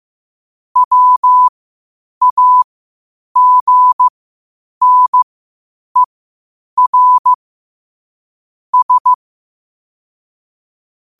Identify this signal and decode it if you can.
Morse code, heard as a keyed tone, about 15 wpm, spaced as for 8 wpm, 1000 Hz. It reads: WAGNER S
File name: qotdfiles / 8wpm